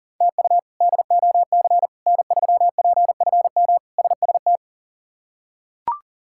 Morse code, heard as a keyed tone, about 40 wpm, 700 Hz. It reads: TU DOC N3PFM SST
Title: K1USN SST QSO Practice - 40wpm